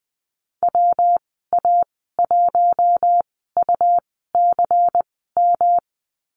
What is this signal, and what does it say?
Morse code, heard as a keyed tone, about 20 wpm, spaced as for 15 wpm, 700 Hz. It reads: WA1UCM
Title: 147 Lesson 24g - Letter y - Call Signs 20wpm